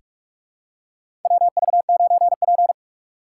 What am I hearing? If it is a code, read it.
Morse code, heard as a keyed tone, about 46 wpm, 690 Hz. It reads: WV9P
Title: USA Call Signs - 4 Characters - 3x Character Spacing - 45wpm